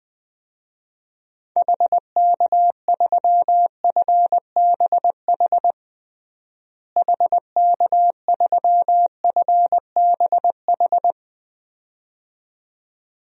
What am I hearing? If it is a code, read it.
Morse code, heard as a keyed tone, about 21 wpm, 700 Hz. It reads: HK3FBH HK3FBH
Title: DXCC-100 Call Signs - 6 Characters - Repeated 2x - 20wpm